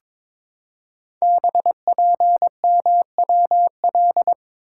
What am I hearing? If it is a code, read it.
Morse code, heard as a keyed tone, about 22 wpm, 700 Hz. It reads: BPMWL
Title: Code Groups - 5 Letters - 22wpm